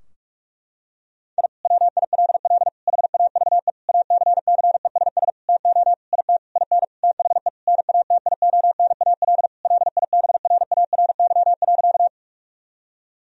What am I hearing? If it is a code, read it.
Morse code, heard as a keyed tone, about 46 wpm, 700 Hz. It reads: I WILL HAVE ACCESS TO IT IN THE NATIONAL LIBRARY.
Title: Sentences from Top 1000 Words - Review Part 4 - 45wpm